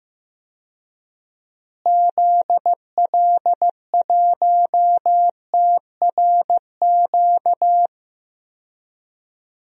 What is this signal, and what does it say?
Morse code, heard as a keyed tone, about 15 wpm, 700 Hz. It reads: ZL1TRQ
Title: DXCC-100 Call Signs - 6 Characters - Repeated 2x - 15wpm